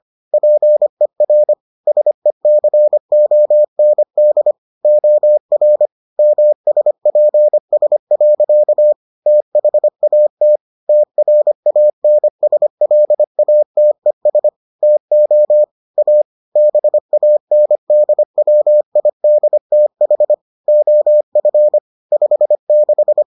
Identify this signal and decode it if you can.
Morse code, heard as a keyed tone, about 25 wpm, 600 Hz. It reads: PER SECOND OR MSPS. THAT TRANSLATES TO A BANDWIDTH OF 56